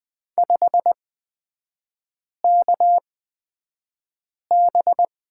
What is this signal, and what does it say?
Morse code, heard as a keyed tone, about 20 wpm, spaced as for 5 wpm, 700 Hz. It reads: 5KB
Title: 250 Lesson 40e - BK - Three Characters 20wpm